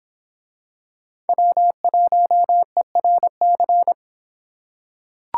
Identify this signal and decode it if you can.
Morse code, heard as a keyed tone, about 26 wpm, 700 Hz. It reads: W1ERC